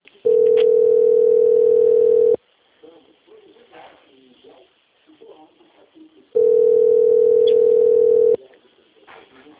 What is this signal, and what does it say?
Ringback tone (North America), 440 + 480 Hz, with speech.
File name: out- - - - - .2753